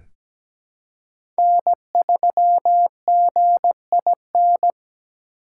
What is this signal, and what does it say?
Morse code, heard as a keyed tone, about 17 wpm, 700 Hz. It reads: N3GIN
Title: USA Call Signs - 5 Characters - 4x Character Spacing - 17wpm